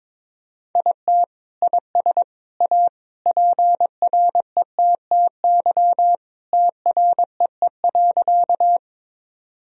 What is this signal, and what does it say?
Morse code, heard as a keyed tone, about 22 wpm, 700 Hz. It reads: IT IS A PRETTY TREE.
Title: Sentences from Top 1000 Words - Review Part 1 - 22wpm